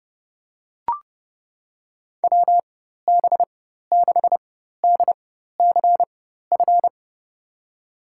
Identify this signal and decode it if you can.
Morse code, heard as a keyed tone, about 30 wpm, spaced as for 14 wpm, 700 Hz. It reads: WB6DCF
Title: USA Call Signs - 6 Characters - 4x Character Spacing - 30wpm